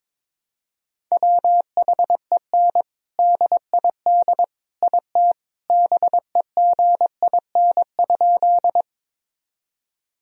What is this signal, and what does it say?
Morse code, heard as a keyed tone, about 22 wpm, 700 Hz. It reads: WHEN DID IT BEGIN?
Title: Sentences from Top 1000 Words - Review Part 1 - 22wpm